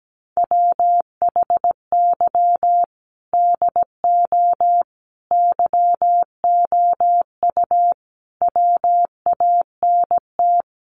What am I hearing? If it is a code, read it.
Morse code, heard as a keyed tone, about 17 wpm, 700 Hz. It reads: WHY DO YOU WANT
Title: Sentences from Top 200 Words 17wpm